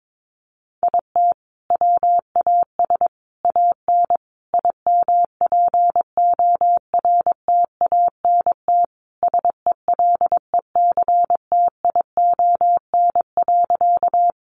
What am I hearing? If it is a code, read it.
Morse code, heard as a keyed tone, about 22 wpm, 700 Hz. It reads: IT WAS AN IMPORTANT SELECTION.
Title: Sentences from Top 1400 Words 22wpm